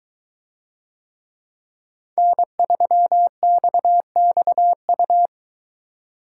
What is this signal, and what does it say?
Morse code, heard as a keyed tone, about 22 wpm, 700 Hz. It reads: N3XXU